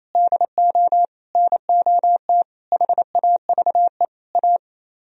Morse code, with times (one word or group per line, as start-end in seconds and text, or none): 0.1-1.1 DO
1.4-2.4 NOT
2.7-4.0 HAVE
4.3-4.6 A